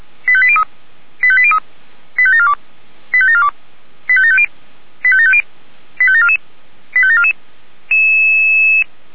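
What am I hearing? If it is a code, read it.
Five-tone selective call (ZVEI), heard as quick runs of five tones, tones of about 70 ms. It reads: 85803 85803 85742 85742 85750 85750 85744 85744
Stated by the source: ZVEI five-tone selective call received on an SDR, recorded by Feuerwehr Orlamünde.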